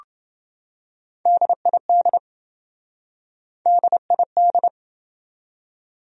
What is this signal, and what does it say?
Morse code, heard as a keyed tone, about 27 wpm, 700 Hz. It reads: DID DID